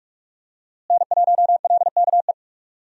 Morse code, heard as a keyed tone, about 45 wpm, 700 Hz. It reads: N1LKE
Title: USA Call Signs - 5 Characters - Repeated 2x - 45wpm